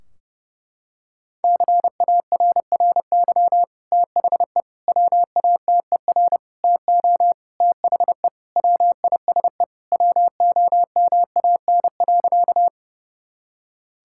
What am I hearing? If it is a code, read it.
Morse code, heard as a keyed tone, about 30 wpm, 700 Hz. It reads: CARRY THE WATER TO THE WISE WOMAN.
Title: Sentences from Top 1600 Words 30wpm